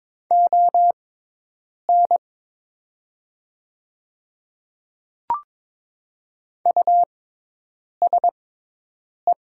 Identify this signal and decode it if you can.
Morse code, heard as a keyed tone, about 22 wpm, spaced as for 8 wpm, 700 Hz. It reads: ON USE
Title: 6x Character Spacing - Top 200 Words - 22wpm